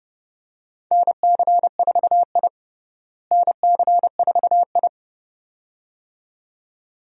Morse code, heard as a keyed tone, about 31 wpm, 700 Hz. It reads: NC4I NC4I
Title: USA Call Signs - 4 Characters - Repeated 2x - 30wpm